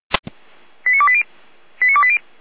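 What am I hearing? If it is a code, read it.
Five-tone selective call (ZVEI), heard as quick runs of five tones, tones of about 70 ms. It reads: 89280 89280
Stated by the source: ZVEI five-tone selective call received on an SDR, fire brigade alerting.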